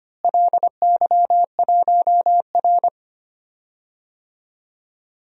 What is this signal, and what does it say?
Morse code, heard as a keyed tone, about 25 wpm, 700 Hz. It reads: LY1R
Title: DXCC-100 Call Signs - 4 Characters - Repeated 2x - 25wpm